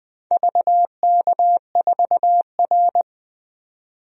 Morse code, heard as a keyed tone, about 20 wpm, 700 Hz. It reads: VK4R
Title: DXCC-100 Call Signs - 4 Characters - 20wpm